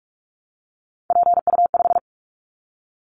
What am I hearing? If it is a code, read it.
Morse code, heard as a keyed tone, about 46 wpm, 700 Hz. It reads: PU5
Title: Callsign Prefixes for Top 100 DXCC - 45wpm